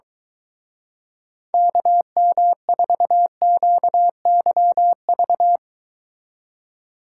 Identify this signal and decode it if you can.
Morse code, heard as a keyed tone, about 23 wpm, 700 Hz. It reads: KM4QYV